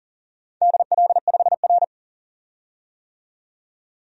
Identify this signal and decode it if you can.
Morse code, heard as a keyed tone, about 40 wpm, 700 Hz. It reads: DL5R
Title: DXCC-100 Call Signs - 4 Characters - 40wpm